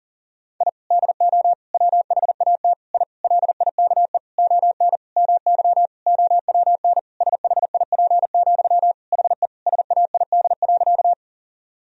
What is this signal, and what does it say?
Morse code, heard as a keyed tone, about 40 wpm, 700 Hz. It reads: I DO WHAT I LIKE ON MY OWN SHIP, HE SAID.
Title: Sentences from Top 1000 Words - Review Part 2 - 40wpm